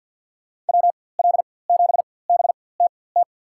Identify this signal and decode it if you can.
Morse code, heard as a keyed tone, about 52 wpm, spaced as for 23 wpm, 700 Hz. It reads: WL7BTT
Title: USA Call Signs - 4x Character Spacing - 50wpm